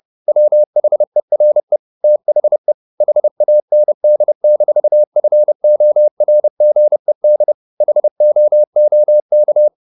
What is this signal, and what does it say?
Morse code, heard as a keyed tone, about 30 wpm, 600 Hz. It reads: WHERE THE HAND-FORGED HOOK
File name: the old man and the sea / the old man and the sea 1 30wpm